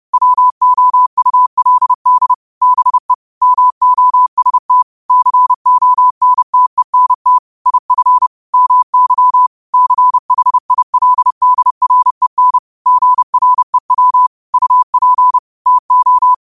Morse code, heard as a keyed tone, about 30 wpm, 1000 Hz. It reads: WOULD BE MOST CONTENT IF MY CHILDREN GREW UP TO